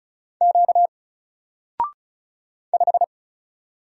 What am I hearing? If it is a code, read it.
Morse code, heard as a keyed tone, about 36 wpm, 700 Hz. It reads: Q 5
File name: Single-Letter-Number-35wpm